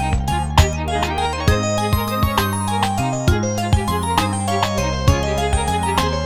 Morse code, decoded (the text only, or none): none